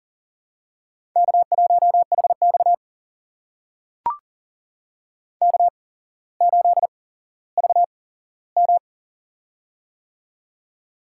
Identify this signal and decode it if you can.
Morse code, heard as a keyed tone, about 40 wpm, spaced as for 17 wpm, 700 Hz. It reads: K1HX K8VM
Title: USA Call Signs - 4 Characters - 8x Character Spacing - 40wpm